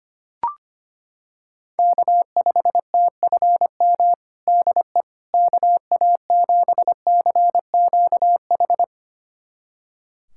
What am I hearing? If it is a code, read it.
Morse code, heard as a keyed tone, about 25 wpm, 700 Hz. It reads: K5TFM DE KA7CQH
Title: USA Callsign DE Callsign 25wpm